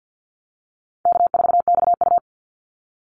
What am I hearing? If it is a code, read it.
Morse code, heard as a keyed tone, about 50 wpm, 700 Hz. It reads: K4XU
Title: Easy Call Signs - 50wpm